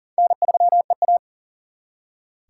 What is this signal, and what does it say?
Morse code, heard as a keyed tone, about 50 wpm, 700 Hz. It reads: N3EA
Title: Speed Racing - USA 1x2 Call Signs 40wpm